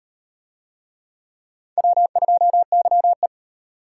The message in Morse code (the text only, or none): W2YE